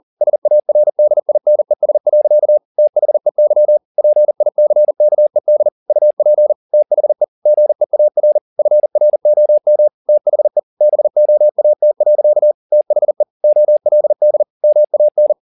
Morse code, heard as a keyed tone, about 40 wpm, 600 Hz. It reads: SARDINES. THEY PICKED UP THE GEAR FROM THE BOAT. THE OLD MAN
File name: the old man and the sea 1 40wpm